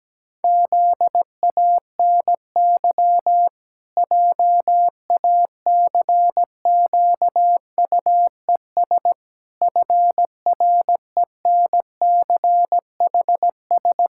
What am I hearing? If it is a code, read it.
Morse code, heard as a keyed tone, about 17 wpm, 700 Hz. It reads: ZANY JACQUES FRENCHS
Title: Pangram Challenge v2 - 17wpm